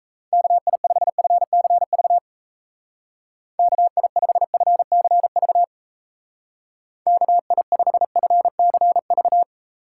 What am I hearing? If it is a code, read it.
Morse code, heard as a keyed tone, about 34 wpm, 700 Hz. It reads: KI5FCV KI5FCV KI5FCV
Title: Speed Racing - USA 2x3 Call Signs 28wpm